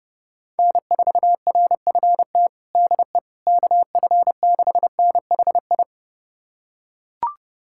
Morse code, heard as a keyed tone, about 30 wpm, 700 Hz. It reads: N4RFT DE KF6NHI
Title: USA Callsign DE Callsign 30wpm